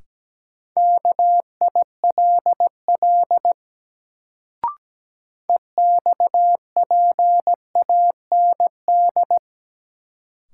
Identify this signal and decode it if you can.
Morse code, heard as a keyed tone, about 17 wpm, 700 Hz. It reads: KILL EXPAND
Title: Sentences from Top 1400 Words - New Word Drill Alt - 17wpm